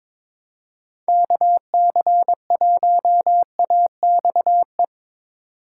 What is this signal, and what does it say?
Morse code, heard as a keyed tone, about 21 wpm, 700 Hz. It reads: KC1AXE
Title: USA Call Signs - 2x Character Spacing - 22wpm